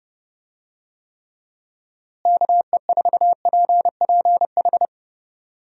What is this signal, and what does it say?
Morse code, heard as a keyed tone, about 31 wpm, 700 Hz. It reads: KE4PPH
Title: USA Call Signs - 6 Characters - Repeated 3x - 30wpm